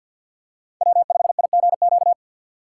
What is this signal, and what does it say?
Morse code, heard as a keyed tone, about 52 wpm, 690 Hz. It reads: W5IGQ
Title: USA Call Signs - 3x Character Spacing - 50wpm